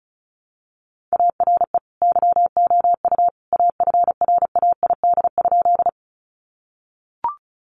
Morse code, heard as a keyed tone, about 34 wpm, 700 Hz. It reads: ARE YOU AFRAID?
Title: Sentences from Top 1500 Words 35wpm